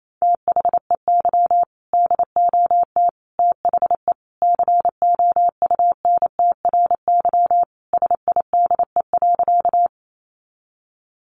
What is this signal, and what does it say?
Morse code, heard as a keyed tone, about 28 wpm, 700 Hz. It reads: THEY DOT THE COUNTRY SIDE.
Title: Sentences from Top 1500 Words 28wpm